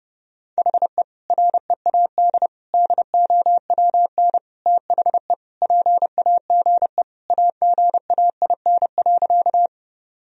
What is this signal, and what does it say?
Morse code, heard as a keyed tone, about 30 wpm, 700 Hz. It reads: HE READ DOWN THE PAGE AGAIN.
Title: Sentences from Top 1000 Words - Review Part 1 - 30wpm